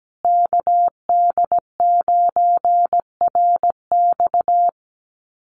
Morse code, heard as a keyed tone, about 17 wpm, 700 Hz. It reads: KD9RX
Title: USA Call Signs Repeated 3x 17wpm